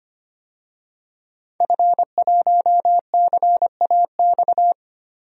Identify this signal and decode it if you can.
Morse code, heard as a keyed tone, about 25 wpm, 700 Hz. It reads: F1CAX
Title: DXCC-100 Call Signs - 5 Characters - Repeated 2x - 25wpm